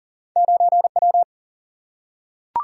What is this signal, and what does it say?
Morse code, heard as a keyed tone, about 39 wpm, 700 Hz. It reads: 9W
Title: Single-Letter Single-Number 40wpm